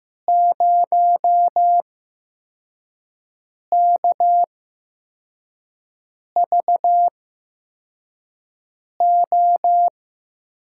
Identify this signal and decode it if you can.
Morse code, heard as a keyed tone, about 15 wpm, spaced as for 4 wpm, 700 Hz. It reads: 0KVO
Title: USA Call Signs - 8x Character Spacing - 15wpm